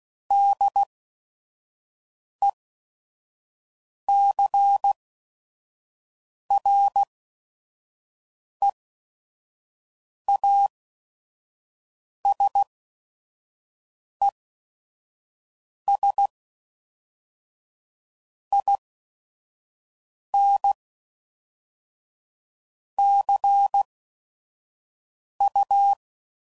Morse code, heard as a keyed tone, about 16 wpm, 800 Hz. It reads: D E C R E A S E S I N C U